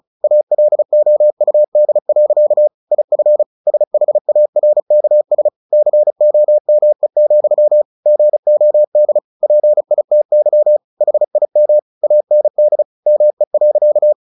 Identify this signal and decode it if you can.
Morse code, heard as a keyed tone, about 34 wpm, 600 Hz. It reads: ALOUD. IF SHARKS COME, GOD PITY HIM AND ME.
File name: the old man and the sea / the old man and the sea 1 35wpm